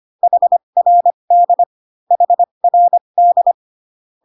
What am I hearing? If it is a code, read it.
Morse code, heard as a keyed tone, about 25 wpm, spaced as for 22 wpm, 700 Hz. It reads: HRD HRD
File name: CW common abbreviations alphabetical repeated 3 times 25 18wpm